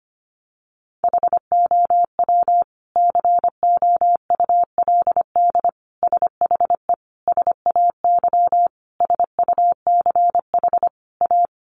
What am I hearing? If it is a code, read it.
Morse code, heard as a keyed tone, about 25 wpm, 700 Hz. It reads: HOW COULD SHE SAY SUCH A
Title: Sentences from Top 200 Words 25wpm